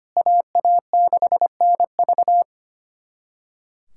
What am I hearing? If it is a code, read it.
Morse code, heard as a keyed tone, about 25 wpm, 700 Hz. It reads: AA6NV
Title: USA Callsign DE Callsign - 5 Characters - 25wpm